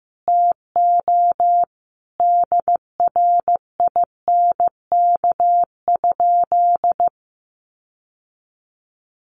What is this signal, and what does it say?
Morse code, heard as a keyed tone, about 15 wpm, 700 Hz. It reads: TO DRINK?
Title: Sentences from Top 1500 Words 15wpm